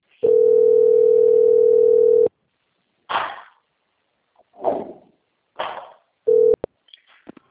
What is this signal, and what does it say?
Ringback tone (North America), 440 + 480 Hz.